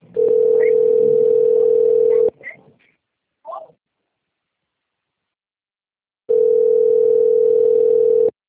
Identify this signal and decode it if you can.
Ringback tone (North America), 440 + 480 Hz, with speech.
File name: rg-1600-unknown- - - .10577